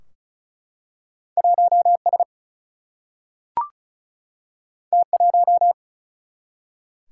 Morse code, heard as a keyed tone, about 34 wpm, 700 Hz. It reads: 1 S T 1
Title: Single-Letter Single-Number 35wpm